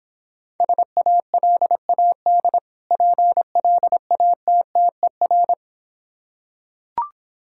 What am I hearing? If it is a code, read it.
Morse code, heard as a keyed tone, about 26 wpm, 690 Hz. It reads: SALAD PLATTER